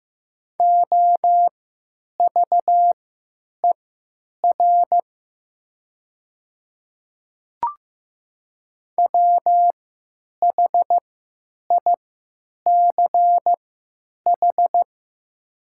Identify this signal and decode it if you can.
Morse code, heard as a keyed tone, about 15 wpm, spaced as for 9 wpm, 700 Hz. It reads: OVER WHICH
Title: 3x Character Spacing - Top 100 Words - 15wpm